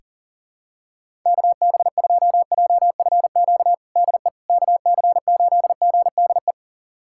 Morse code, heard as a keyed tone, about 40 wpm, 700 Hz. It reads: KB2JFQ DE KC8GDE
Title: USA Callsign DE Callsign - 6 Characters - 40wpm